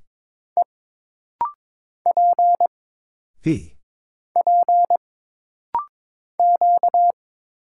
Morse code, heard as a keyed tone, about 22 wpm, 700 Hz. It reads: E P P Q
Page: This rapid-fire timing is geared to push your ICR (Instant Character Recognition) skills.